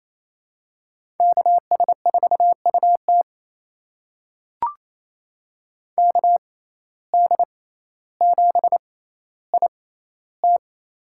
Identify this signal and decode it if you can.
Morse code, heard as a keyed tone, about 28 wpm, spaced as for 14 wpm, 700 Hz. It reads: KS4UT KD7IT
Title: USA Call Signs - 5 Characters - 6x Character Spacing - 28wpm